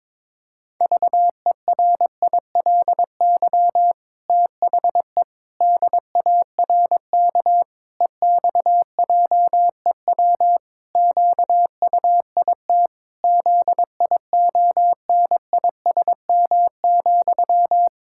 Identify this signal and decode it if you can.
Morse code, heard as a keyed tone, about 22 wpm, 700 Hz. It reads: VERILY THE DARK EXJEW QUIT ZIONISM,
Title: Pangram Challenge v2 - 22wpm